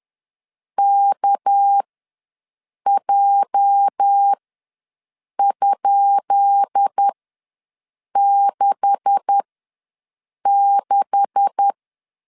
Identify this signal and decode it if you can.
Morse code, heard as a keyed tone, about 10 wpm, spaced as for 6 wpm, 800 Hz. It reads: KJ?66
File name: LEZIONI / 10 SIGNS / 10B INT ws 10wpm 4134